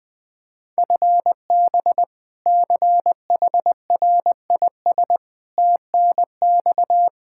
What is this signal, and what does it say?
Morse code, heard as a keyed tone, about 20 wpm, 700 Hz. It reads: FB CHRIS TNX
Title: Practice QSO - Exchange 3 - More Info - 20wpm